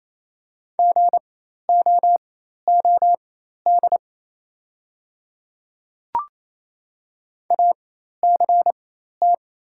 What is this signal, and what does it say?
Morse code, heard as a keyed tone, about 27 wpm, spaced as for 13 wpm, 690 Hz. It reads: GOOD ACT
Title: 4x Character Spacing - Top 200 Words - 28wpm